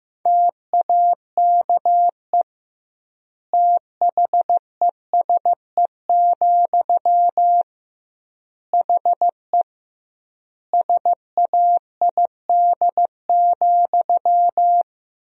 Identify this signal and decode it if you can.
Morse code, heard as a keyed tone, about 15 wpm, 700 Hz. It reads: TAKE THESE, HE SAID,
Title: 2x Word-Spacing - Sentences from Top 300 Words - 15wpm